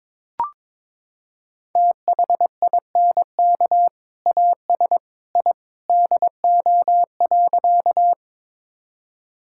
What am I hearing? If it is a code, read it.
Morse code, heard as a keyed tone, about 22 wpm, 700 Hz. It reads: THINK AS I DO.
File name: practice / 22wpm / Sentences with 2-Word Phrases - 22wpm